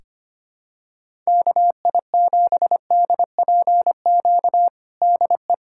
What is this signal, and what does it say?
Morse code, heard as a keyed tone, about 25 wpm, 700 Hz. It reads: KI7DPQ DE
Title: USA Callsign DE Callsign - 6 Characters - 25wpm